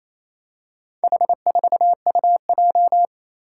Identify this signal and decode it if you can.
Morse code, heard as a keyed tone, about 28 wpm, 700 Hz. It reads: H4UJ
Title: Four Challenging Letter-Number Combinations 28wpm